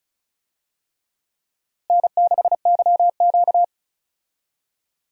Morse code, heard as a keyed tone, about 34 wpm, 700 Hz. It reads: N6YQ
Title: USA Call Signs - 4 Characters - Repeated 3x - 35wpm